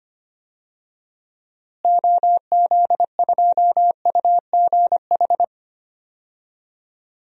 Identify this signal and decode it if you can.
Morse code, heard as a keyed tone, about 25 wpm, 700 Hz. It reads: OZ2UGH
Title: DXCC-100 Call Signs - 6 Characters - Repeated 3x - 25wpm